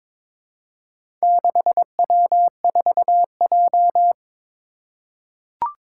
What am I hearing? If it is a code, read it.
Morse code, heard as a keyed tone, about 22 wpm, 700 Hz. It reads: 6W4J
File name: practice / 22wpm / Four Letter-Number Combinations - 22wpm